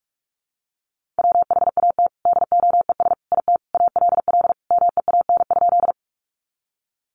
Morse code, heard as a keyed tone, about 44 wpm, 700 Hz. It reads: WHAT DOES IT ALL MEAN?
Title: Sentences from Top 200 Words 45wpm